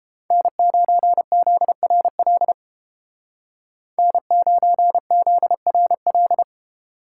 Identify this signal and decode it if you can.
Morse code, heard as a keyed tone, about 30 wpm, 700 Hz. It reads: N9ZRL N9ZRL